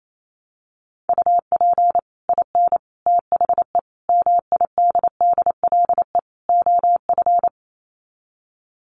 Morse code, heard as a keyed tone, about 28 wpm, 700 Hz. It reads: UP IN THE MIDDLE OF
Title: Sets of 5 Words Encore 28wpm